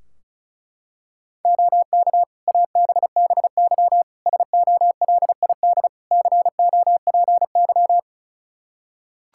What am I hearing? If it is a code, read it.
Morse code, heard as a keyed tone, about 36 wpm, 700 Hz. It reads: OK ABBY SOLID COPY
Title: Practice QSO - Exchange 3 - More Info - 35wpm